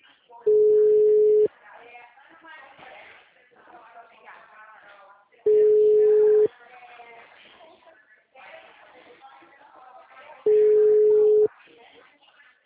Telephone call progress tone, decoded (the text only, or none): ringback tone